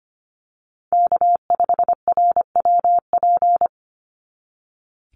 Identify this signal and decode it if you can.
Morse code, heard as a keyed tone, about 25 wpm, 700 Hz. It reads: K5RWP
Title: NA SOTA Chaser Callsigns 25wpm